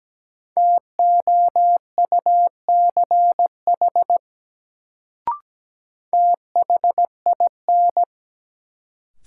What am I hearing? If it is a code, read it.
Morse code, heard as a keyed tone, about 17 wpm, 700 Hz. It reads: TOUCH THIN
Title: Sentences from Top 1200 Words - New Word Drill - 17wpm